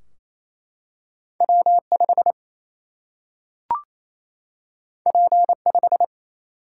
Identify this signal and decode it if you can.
Morse code, heard as a keyed tone, about 28 wpm, 700 Hz. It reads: W 5 P 5
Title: Single-Letter Single-Number 28wpm